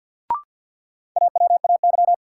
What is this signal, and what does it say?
Morse code, heard as a keyed tone, about 53 wpm, 700 Hz. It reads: AWAY
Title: Sentences from Top 600 Words - New Word Drill - 50wpm